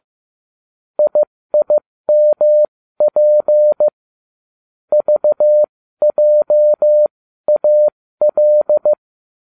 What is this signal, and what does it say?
Morse code, heard as a keyed tone, about 15 wpm, spaced as for 13 wpm, 600 Hz. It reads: IIMP VJAL
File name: cw / ubacw20 2